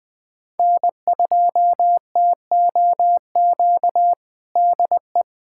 Morse code, heard as a keyed tone, about 20 wpm, 700 Hz. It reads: N2TOQ DE
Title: USA Callsign DE Callsign - 5 Characters - 20wpm